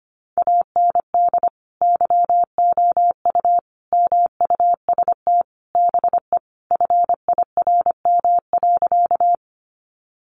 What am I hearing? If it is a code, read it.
Morse code, heard as a keyed tone, about 25 wpm, 700 Hz. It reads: AND YOU MUST BE FIRM.
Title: Sentences from Top 1400 Words 25wpm